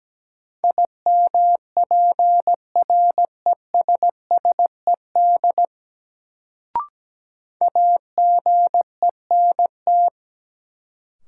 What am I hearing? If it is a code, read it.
Morse code, heard as a keyed tone, about 17 wpm, 700 Hz. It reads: IMPRESSED AGENT
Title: Sentences from Top 1600 Words - New Word Drill - 17wpm